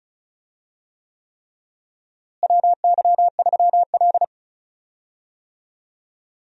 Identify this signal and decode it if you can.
Morse code, heard as a keyed tone, about 36 wpm, 700 Hz. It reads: WY3L
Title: USA Call Signs - 4 Characters - Repeated 3x - 35wpm